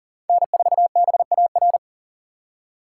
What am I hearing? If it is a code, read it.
Morse code, heard as a keyed tone, about 40 wpm, 700 Hz. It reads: N4BAR
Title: USA Call Signs - 4x Character Spacing - 40wpm